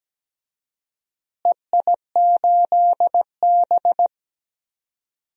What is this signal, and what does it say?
Morse code, heard as a keyed tone, about 17 wpm, 700 Hz. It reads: EI8B
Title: DXCC-100 Call Signs - 4 Characters - Repeated 3x - 17wpm